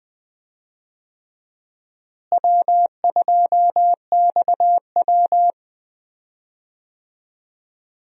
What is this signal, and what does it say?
Morse code, heard as a keyed tone, about 20 wpm, 700 Hz. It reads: W2XW
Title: USA Call Signs - 4 Characters - Repeated 3x - 20wpm